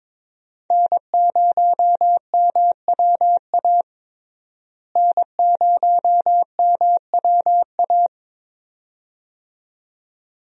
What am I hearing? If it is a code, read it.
Morse code, heard as a keyed tone, about 22 wpm, 700 Hz. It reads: N0MWA N0MWA
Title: USA Call Signs - 5 Characters - Repeated 2x - 22wpm